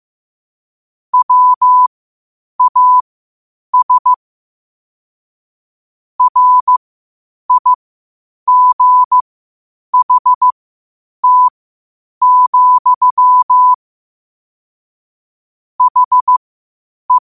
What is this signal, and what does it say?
Morse code, heard as a keyed tone, about 15 wpm, spaced as for 8 wpm, 1000 Hz. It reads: WAS RIGHT, HE
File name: qotdfiles / 8wpm